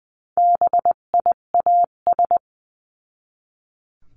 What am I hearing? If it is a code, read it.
Morse code, heard as a keyed tone, about 20 wpm, 700 Hz. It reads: BIAS
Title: 258 Lesson 42c - Words 16-20wpm